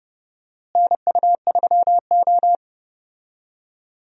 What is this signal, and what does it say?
Morse code, heard as a keyed tone, about 32 wpm, 700 Hz. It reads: NU3O
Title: USA Call Signs - 4 Characters - Repeated 3x - 30wpm